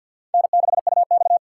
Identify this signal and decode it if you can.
Morse code, heard as a keyed tone, about 52 wpm, 700 Hz. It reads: N6UX
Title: USA Call Signs - 6x Character Spacing - 50wpm